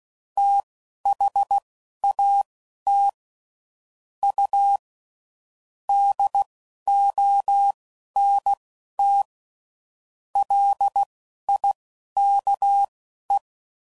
Morse code, heard as a keyed tone, about 16 wpm, spaced as for 11 wpm, 800 Hz. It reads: THAT U DONT LIKE
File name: CW Code Groups @10WPM / QSOs / QSO WD0FBT